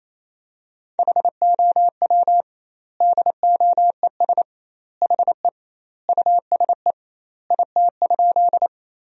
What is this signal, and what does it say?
Morse code, heard as a keyed tone, about 28 wpm, 700 Hz. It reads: HOW DOES HE USE IT?
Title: 2x Word-Spacing - Sentences from Top 200 Words - 28wpm